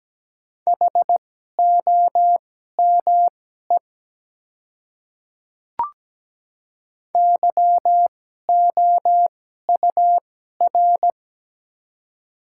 Morse code, heard as a keyed tone, about 17 wpm, spaced as for 12 wpm, 700 Hz. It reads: HOME YOUR
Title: 2x Character Spacing - Top 200 Words - 17wpm